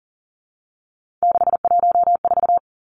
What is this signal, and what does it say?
Morse code, heard as a keyed tone, about 42 wpm, 700 Hz. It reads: 614
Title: Three Challenging Letter-Number Combinations 40wpm